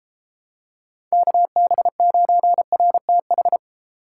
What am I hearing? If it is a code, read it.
Morse code, heard as a keyed tone, about 32 wpm, 700 Hz. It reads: KB9RTH